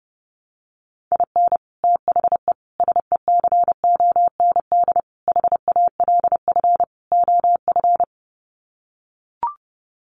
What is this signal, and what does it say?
Morse code, heard as a keyed tone, about 30 wpm, 700 Hz. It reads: IN THE SECOND HALF OF
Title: Sets of 5 Words 30wpm